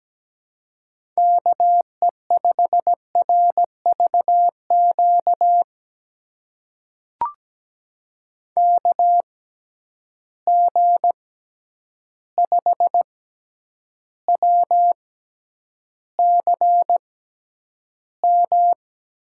Morse code, heard as a keyed tone, about 17 wpm, spaced as for 9 wpm, 700 Hz. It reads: KE5RVQ KG5WCM
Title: USA Call Signs - 6 Characters - 6x Character Spacing - 17wpm